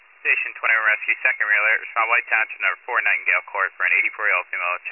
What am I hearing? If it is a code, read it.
Emergency Alert System (SAME) tones, heard as harsh buzzing bursts, with speech. It is heard from 0.6 to 4.3 s.